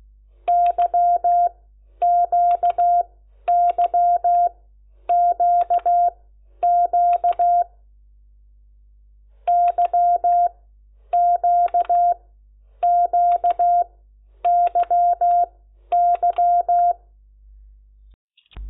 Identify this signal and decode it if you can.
Morse code, heard as a keyed tone, about 16 wpm, spaced as for 10 wpm, 700 Hz. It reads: YQYQQ YQQYY
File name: Lecciones / CLASE Nº10